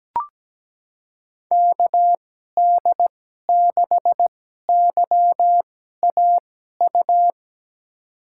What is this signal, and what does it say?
Morse code, heard as a keyed tone, about 17 wpm, spaced as for 12 wpm, 700 Hz. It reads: KD6YAU
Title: USA Call Signs - 2x Character Spacing - 17wpm